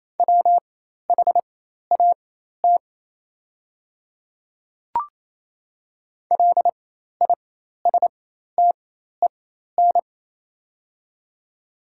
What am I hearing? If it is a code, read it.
Morse code, heard as a keyed tone, about 28 wpm, spaced as for 13 wpm, 700 Hz. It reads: WHAT LISTEN
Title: 4x Character Spacing - Top 500 Words - 28wpm